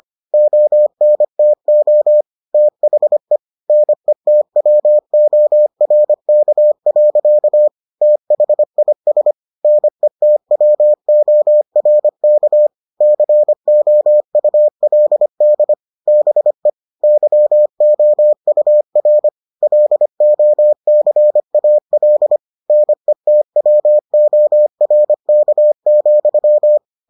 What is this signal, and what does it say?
Morse code, heard as a keyed tone, about 25 wpm, 600 Hz. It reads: ONTO THE NETWORK. THIS NETWORK COULD BE YOUR LOCAL NETWORK,